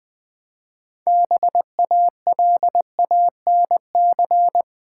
Morse code, heard as a keyed tone, about 20 wpm, 700 Hz. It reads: BALANC
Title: Build A Word - 3 Letters At A Time - 8-Letter Words - 20wpm